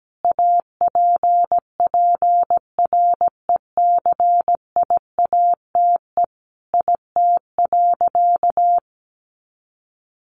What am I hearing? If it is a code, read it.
Morse code, heard as a keyed tone, about 17 wpm, 700 Hz. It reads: APPRECIATE IT.
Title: Sentences from Top 1300 Words 17wpm